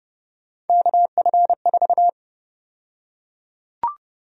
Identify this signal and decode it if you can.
Morse code, heard as a keyed tone, about 31 wpm, 700 Hz. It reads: KF4
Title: USA Call Sign Prefixes - Repeated 2x - 30wpm